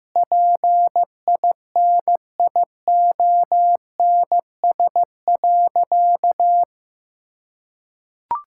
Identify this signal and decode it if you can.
Morse code, heard as a keyed tone, about 15 wpm, 700 Hz. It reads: PINIONS.
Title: Pangram Challenge v2 - 15wpm